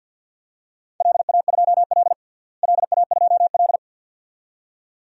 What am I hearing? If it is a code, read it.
Morse code, heard as a keyed tone, about 52 wpm, 700 Hz. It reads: LA2L LA2L
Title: DXCC-100 Call Signs - 4 Characters - Repeated 2x - 50wpm